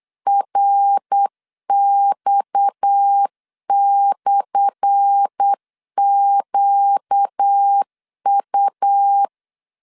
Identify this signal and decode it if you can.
Morse code, heard as a keyed tone, about 8 wpm, 800 Hz. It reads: RX/QU
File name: LEZIONI / 10 SIGNS / 10A SLASH 8wpm 3452